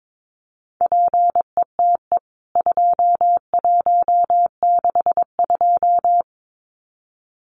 Morse code, heard as a keyed tone, about 22 wpm, 700 Hz. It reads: PETE 2162
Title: CWT Exchange 22wpm